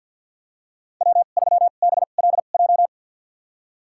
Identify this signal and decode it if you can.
Morse code, heard as a keyed tone, about 52 wpm, spaced as for 36 wpm, 700 Hz. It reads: W3BLJ
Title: USA Call Signs - 5 Characters - 2x Character Spacing - 50wpm